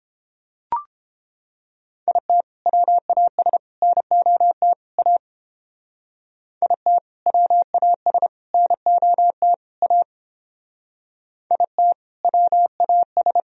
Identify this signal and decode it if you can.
Morse code, heard as a keyed tone, about 31 wpm, 700 Hz. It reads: IT WAS NOT A IT WAS NOT A IT WAS